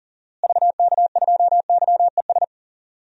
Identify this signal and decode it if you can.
Morse code, heard as a keyed tone, about 40 wpm, 700 Hz. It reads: VK2YES
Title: DXCC-100 Call Signs - 6 Characters - Repeated 2x - 40wpm